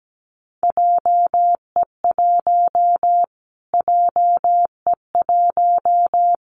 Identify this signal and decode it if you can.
Morse code, heard as a keyed tone, about 17 wpm, 700 Hz. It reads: JE1 JE1
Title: Callsign Prefixes for Top 25 DXCC - Repeated 3x - 17wpm